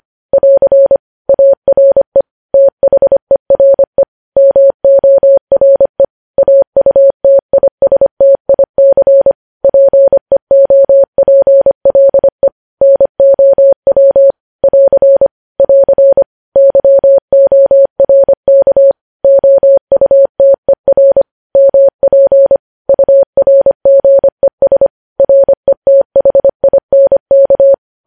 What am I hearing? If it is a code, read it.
Morse code, heard as a keyed tone, about 25 wpm, 550 Hz. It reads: <AR> ARE THERE MORE AUTISTIC PEOPLE NOW <AR> <AR> YORK OUTER MP URGES RETHINK
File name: podcast / News headlines-1x-25x25